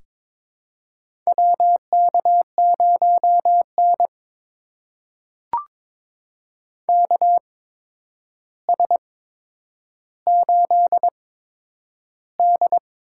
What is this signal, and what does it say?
Morse code, heard as a keyed tone, about 22 wpm, spaced as for 9 wpm, 700 Hz. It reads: WK0N KS8D
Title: USA Call Signs - 4 Characters - 8x Character Spacing - 22wpm